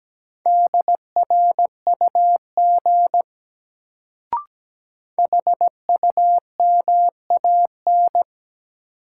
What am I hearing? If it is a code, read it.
Morse code, heard as a keyed tone, about 17 wpm, 700 Hz. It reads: DRUG HUMAN
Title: Sets of 5 Words - New Word Drill - 17wpm